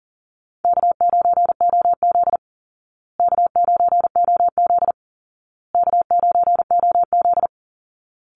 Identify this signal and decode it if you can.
Morse code, heard as a keyed tone, about 40 wpm, 700 Hz. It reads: K9OZ K9OZ K9OZ
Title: Easy Call Signs Repeated 3x - 40wpm